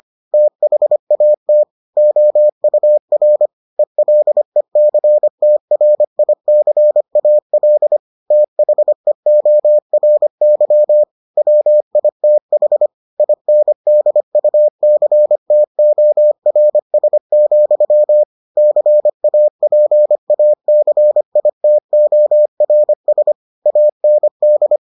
Morse code, heard as a keyed tone, about 25 wpm, 600 Hz. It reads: THAT OUR ELECTRICAL THEORY WITH INDUCTORS, CAPACITORS AND